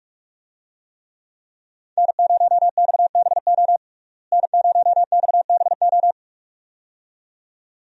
Morse code, heard as a keyed tone, about 43 wpm, 700 Hz. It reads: N0XBO N0XBO
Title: USA Call Signs - 5 Characters - Repeated 2x - 45wpm